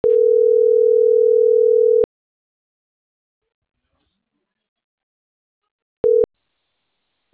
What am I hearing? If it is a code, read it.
Ringback tone (North America), 440 + 480 Hz.